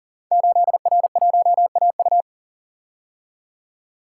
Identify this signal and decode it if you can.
Morse code, heard as a keyed tone, about 38 wpm, 700 Hz. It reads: 8R1AU
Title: Code Groups - 5 Letter:Numbers - 40wpm